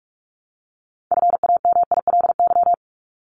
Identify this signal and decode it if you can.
Morse code, heard as a keyed tone, about 44 wpm, 700 Hz. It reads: FAMILY
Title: Top 500 Words 45wpm